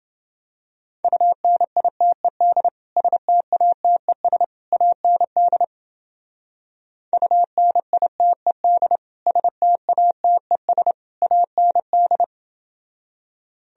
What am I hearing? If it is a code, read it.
Morse code, heard as a keyed tone, about 29 wpm, 700 Hz. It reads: UNITED STATES AND UNITED STATES AND